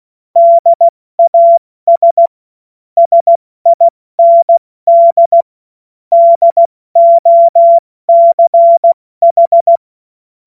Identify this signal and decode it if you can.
Morse code, heard as a keyed tone, about 16 wpm, 680 Hz. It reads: DAS SIND DOCH
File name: Witz-30-September-2025-70-F80